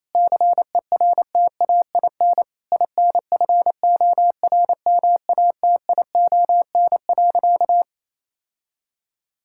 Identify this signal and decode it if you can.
Morse code, heard as a keyed tone, about 28 wpm, 700 Hz. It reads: CERTAIN INFORMATION.
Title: Sentences from Top 1000 Words - Review Part 6 - 28wpm